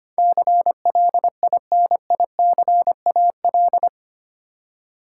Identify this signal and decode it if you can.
Morse code, heard as a keyed tone, about 25 wpm, 700 Hz. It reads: CLINICAL
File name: 8 Letter Words - First 3 Letters Given - 25wpm